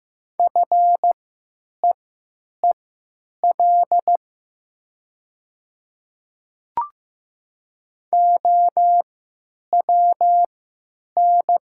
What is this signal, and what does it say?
Morse code, heard as a keyed tone, about 15 wpm, spaced as for 9 wpm, 700 Hz. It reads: FEEL OWN
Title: 3x Character Spacing - Top 500 Words - 15wpm